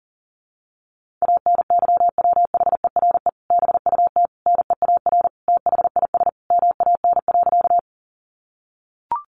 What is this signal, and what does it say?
Morse code, heard as a keyed tone, about 37 wpm, 700 Hz. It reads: ANYWHERE BUT NEAR THIS MAN.
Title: Sentences from Top 1400 Words 40wpm